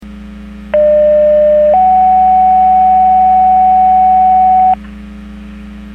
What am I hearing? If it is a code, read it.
Two-tone page, heard as a one-second tone, then a three-second tone, about 600 then 750 Hz. It is heard from 0.7 to 4.7 s.
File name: Portage / 06104 11-09 Portage 154.385 Paging